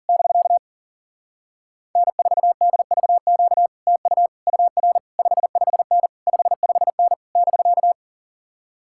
Morse code, heard as a keyed tone, about 40 wpm, 700 Hz. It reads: <BK> N4DVQ TU UR 55N 55N <BK>
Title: Speed-Racing Common QSO Phrases - 40wpm